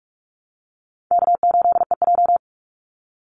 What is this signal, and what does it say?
Morse code, heard as a keyed tone, about 44 wpm, 700 Hz. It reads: K8EJ
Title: Easy Call Signs Repeated 3x - 45wpm